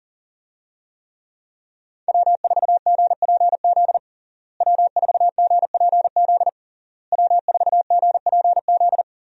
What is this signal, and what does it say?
Morse code, heard as a keyed tone, about 40 wpm, 700 Hz. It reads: W4GPZ W4GPZ W4GPZ
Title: USA Call Signs - 5 Characters - Repeated 3x - 40wpm